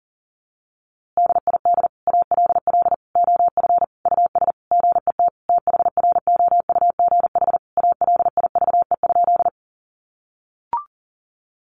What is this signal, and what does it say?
Morse code, heard as a keyed tone, about 40 wpm, 700 Hz. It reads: DID ALL OF US GET THROUGH ALIVE?
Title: A Sound Of Thunder 40wpm